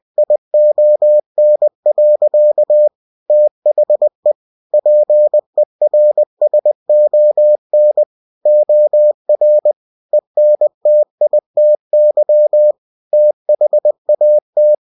Morse code, heard as a keyed tone, about 20 wpm, 600 Hz. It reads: ION. THE PERSON OR ENTITY THAT
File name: cw-books / winnie-the-pooh / winnie-20wpm-page-0119